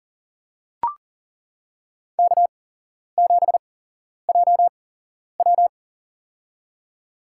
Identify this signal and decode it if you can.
Morse code, heard as a keyed tone, about 40 wpm, spaced as for 11 wpm, 700 Hz. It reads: K7JW
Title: USA Call Signs - 4 Characters - 8x Character Spacing - 40wpm